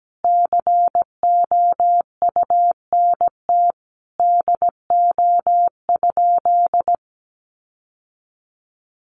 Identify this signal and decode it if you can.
Morse code, heard as a keyed tone, about 17 wpm, 700 Hz. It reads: COUNT DO?
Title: Sentences from Top 1500 Words 17wpm